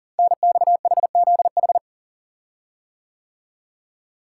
Morse code, heard as a keyed tone, about 38 wpm, 700 Hz. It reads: NXHZH
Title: Code Groups - 5 Letters - 40wpm